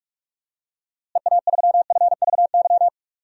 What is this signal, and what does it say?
Morse code, heard as a keyed tone, about 42 wpm, 700 Hz. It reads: EA3FVY
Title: DXCC-100 Call Signs - 6 Characters - Repeated 3x - 45wpm